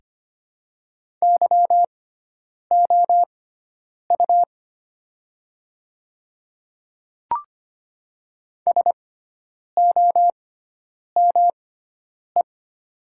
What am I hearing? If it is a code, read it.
Morse code, heard as a keyed tone, about 25 wpm, spaced as for 9 wpm, 700 Hz. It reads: YOU SOME
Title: 6x Character Spacing - Top 100 Words - 25wpm